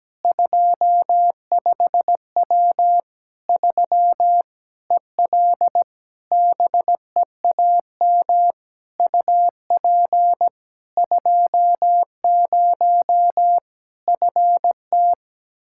Morse code, heard as a keyed tone, about 17 wpm, 700 Hz. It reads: 25W 3 EL BEAM UP 20 FT
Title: Practice QSO - Exchange 2 - Info - 17wpm